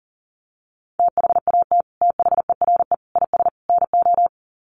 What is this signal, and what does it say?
Morse code, heard as a keyed tone, about 40 wpm, 700 Hz. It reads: THAT THERE IS NO
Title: Sets of 4 Words 40wpm